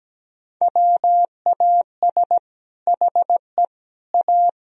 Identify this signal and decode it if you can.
Morse code, heard as a keyed tone, about 17 wpm, 700 Hz. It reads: WAS HE A
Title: Sentences from Top 1000 Words - Review Part 2 - 17wpm